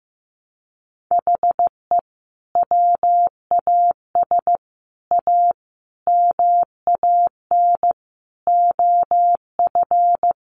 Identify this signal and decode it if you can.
Morse code, heard as a keyed tone, about 15 wpm, 700 Hz. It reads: HE WAS A MAN OF
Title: Sentences from Top 1200 Words 15wpm